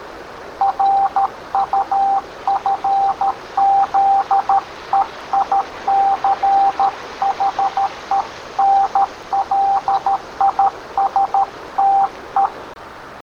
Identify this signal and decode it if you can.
Morse code, heard as a keyed tone, about 13 wpm, 790 Hz, with noise. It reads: RUFZEICHENLISTE